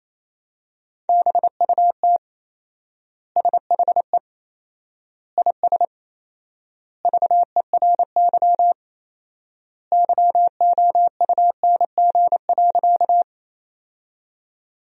Morse code, heard as a keyed tone, about 28 wpm, 700 Hz. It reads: BUT SHE IS VERY YOUNG.
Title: 4x Word-Spacing - Sentences from Top 400 Words - 28wpm